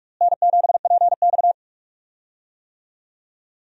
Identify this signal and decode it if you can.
Morse code, heard as a keyed tone, about 46 wpm, 700 Hz. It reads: N7PX
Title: USA Call Signs - 4 Characters - Repeated 2x - 45wpm